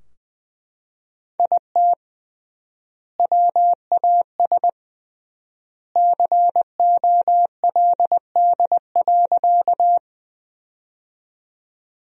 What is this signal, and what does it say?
Morse code, heard as a keyed tone, about 20 wpm, 700 Hz. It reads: IT WAS COLD.
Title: 3x Word-Spacing - Sentences from Top 400 Words - 20wpm